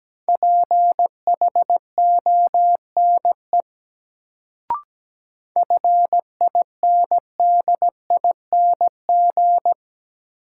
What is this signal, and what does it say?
Morse code, heard as a keyed tone, about 17 wpm, 700 Hz. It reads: PHONE FINDING
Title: Sentences from Top 1100 Words - New Word Drill Alt - 17wpm